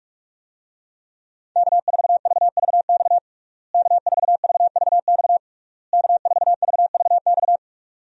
Morse code, heard as a keyed tone, about 46 wpm, 700 Hz. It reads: K4VVX K4VVX K4VVX
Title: USA Call Signs - 5 Characters - Repeated 3x - 45wpm